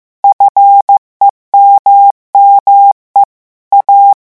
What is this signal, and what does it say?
Morse code, heard as a keyed tone, about 15 wpm, 800 Hz. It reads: FEMME A
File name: morse mp3 / revision9 citations humoristiques vitesse 14 mots minute